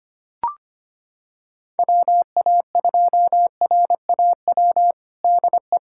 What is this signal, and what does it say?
Morse code, heard as a keyed tone, about 25 wpm, 700 Hz. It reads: WA2RAW DE
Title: USA Callsign DE Callsign - 6 Characters - 25wpm